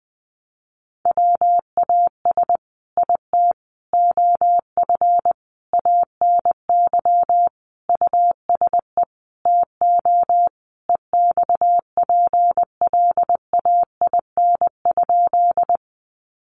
Morse code, heard as a keyed tone, about 20 wpm, 700 Hz. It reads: WAS IT OF ANY USE TO EXPLAIN?
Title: Sentences from Top 1100 Words 20wpm